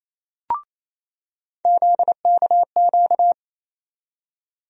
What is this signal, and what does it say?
Morse code, heard as a keyed tone, about 28 wpm, 700 Hz. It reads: ZKQ
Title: USA Call Sign Postfixes - 3 Letters 28wpm